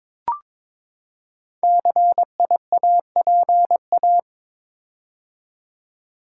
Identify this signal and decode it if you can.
Morse code, heard as a keyed tone, about 22 wpm, 700 Hz. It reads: CIAPA
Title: Code Groups - 5 Letters - 22wpm